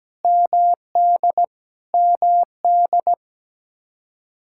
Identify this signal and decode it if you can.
Morse code, heard as a keyed tone, about 17 wpm, 700 Hz. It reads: MD MD
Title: K1USN SST QSO Practice 17wpm